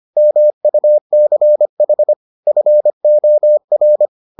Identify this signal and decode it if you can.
Morse code, heard as a keyed tone, about 25 wpm, 600 Hz. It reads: MUCH FOR